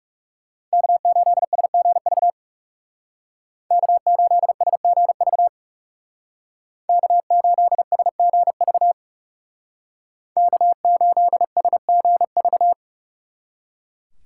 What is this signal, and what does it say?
Morse code, heard as a keyed tone, about 36 wpm, 700 Hz. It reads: K8SGV K8SGV K8SGV K8SGV
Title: Speed Racing - USA 1x3 Call Signs 30wpm